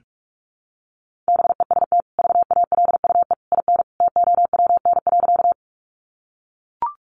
Morse code, heard as a keyed tone, about 44 wpm, 700 Hz. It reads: BEST VALUE IN TOWN.
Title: Sentences from Top 600 Words 45wpm